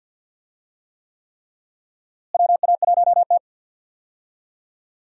Morse code, heard as a keyed tone, about 52 wpm, 700 Hz. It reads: WA1T
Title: USA Call Signs - 4 Characters - Repeated 3x - 50wpm